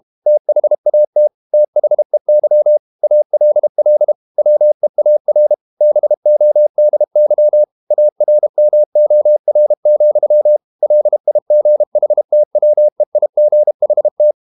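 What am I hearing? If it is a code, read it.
Morse code, heard as a keyed tone, about 32 wpm, 600 Hz. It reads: THAT THEY ALL WEAR BODY ARMOR, LIGHTWEIGHT
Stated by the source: Morse code practice at 32 wpm from CW med Gnister, reading Ken Follett's novel Never in English.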